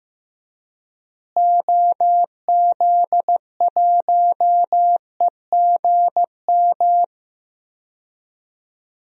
Morse code, heard as a keyed tone, about 15 wpm, 700 Hz. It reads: OZ1EGM
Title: DXCC-100 Call Signs - 6 Characters - Repeated 3x - 15wpm